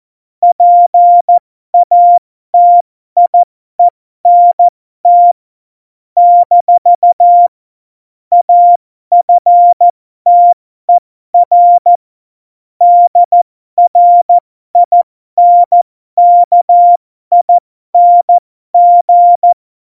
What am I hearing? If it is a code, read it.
Morse code, heard as a keyed tone, about 14 wpm, spaced as for 12 wpm, 700 Hz. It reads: PATIENT - AFTER DRINKING
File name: Morsekurs / Witze